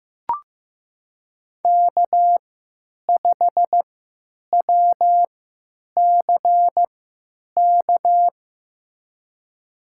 Morse code, heard as a keyed tone, about 15 wpm, spaced as for 8 wpm, 700 Hz. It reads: K5WCK
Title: USA Call Signs - 5 Characters - 3x Character Spacing - 15wpm